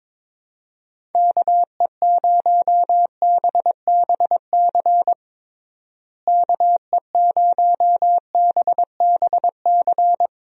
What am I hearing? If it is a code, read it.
Morse code, heard as a keyed tone, about 22 wpm, 700 Hz. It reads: KE0BBC KE0BBC
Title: USA Call Signs - 6 Characters - Repeated 2x - 22wpm